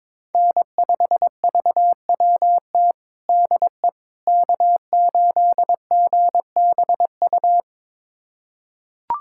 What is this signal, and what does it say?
Morse code, heard as a keyed tone, about 22 wpm, 700 Hz. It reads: N5VWT DE K8GBU
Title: USA Callsign DE Callsign - 5 Characters - 22wpm